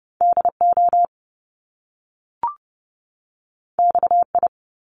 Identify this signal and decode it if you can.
Morse code, heard as a keyed tone, about 30 wpm, 700 Hz. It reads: DO XI
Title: Top Two Letter Combinations in Words 30wpm